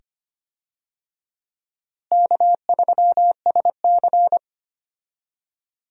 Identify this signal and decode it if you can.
Morse code, heard as a keyed tone, about 25 wpm, 700 Hz. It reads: K3SC
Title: USA Call Signs - 4 Characters - Repeated 2x - 25wpm